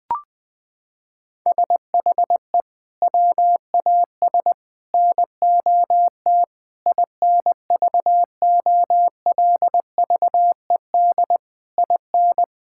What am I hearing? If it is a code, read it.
Morse code, heard as a keyed tone, about 20 wpm, 700 Hz. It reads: SHE WAS NOT INVOLVED IN
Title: Sentences with 6-Word Phrases - 20wpm